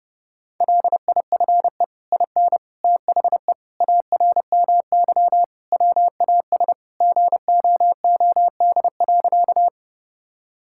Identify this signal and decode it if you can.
Morse code, heard as a keyed tone, about 30 wpm, 700 Hz. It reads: LIFE IN THE ARMY WAS GOOD.
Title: Sentences from Top 1000 Words - Review Part 5 - 30wpm